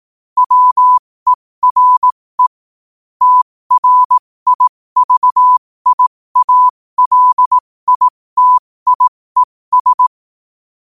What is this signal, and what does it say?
Morse code, heard as a keyed tone, about 18 wpm, spaced as for 15 wpm, 1000 Hz. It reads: WERE TRIVIALITIES
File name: qotdfiles / 15wpm